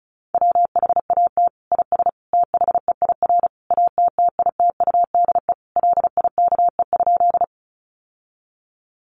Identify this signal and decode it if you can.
Morse code, heard as a keyed tone, about 34 wpm, 700 Hz. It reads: WHAT IS THEIR ATTITUDE LIKE?
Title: Sentences from Top 1300 Words 35wpm